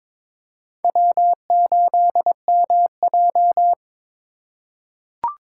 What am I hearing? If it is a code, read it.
Morse code, heard as a keyed tone, about 22 wpm, 700 Hz. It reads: W8MJ
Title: USA Call Signs - 3x Character Spacing - 22wpm